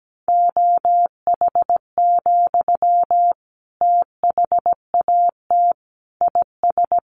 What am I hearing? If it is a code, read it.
Morse code, heard as a keyed tone, about 17 wpm, 700 Hz. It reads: OH, THAT IS
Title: Sentences from Top 500 Words 17wpm